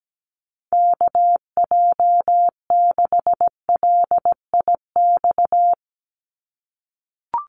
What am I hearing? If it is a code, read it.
Morse code, heard as a keyed tone, about 17 wpm, 700 Hz. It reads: KJ6LIX
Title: USA Call Signs 17wpm